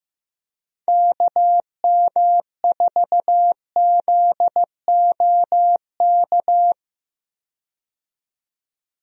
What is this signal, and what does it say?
Morse code, heard as a keyed tone, about 15 wpm, 700 Hz. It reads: KM4ZOK
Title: USA Call Signs - 6 Characters - 15wpm